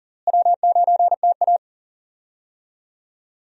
Morse code, heard as a keyed tone, about 39 wpm, 700 Hz. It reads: W9TA
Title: USA Call Signs - 4 Characters - Repeated 3x - 40wpm